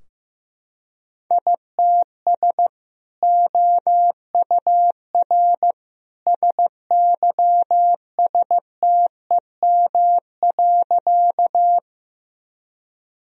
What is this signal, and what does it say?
Morse code, heard as a keyed tone, about 15 wpm, 700 Hz. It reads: ITS OUR SYSTEM.
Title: Sentences from Top 1000 Words - Review Part 3 - 15wpm